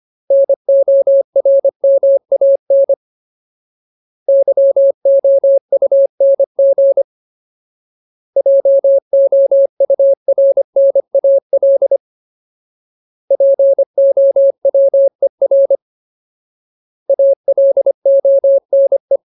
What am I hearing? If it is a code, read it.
Morse code, heard as a keyed tone, about 25 wpm, 550 Hz. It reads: NORMAN YOUNG JOURNAL POWER ALONE
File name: Ultimate Word List / 25WPM Ultimate